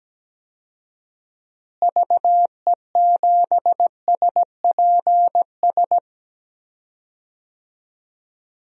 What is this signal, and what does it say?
Morse code, heard as a keyed tone, about 17 wpm, 700 Hz. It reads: VE7SPS
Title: DXCC-100 Call Signs - 6 Characters - Repeated 2x - 17wpm